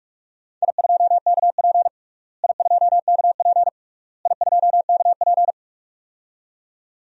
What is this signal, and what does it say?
Morse code, heard as a keyed tone, about 47 wpm, 700 Hz. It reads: I2KP I2KP I2KP
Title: DXCC-100 Call Signs - 4 Characters - Repeated 3x - 45wpm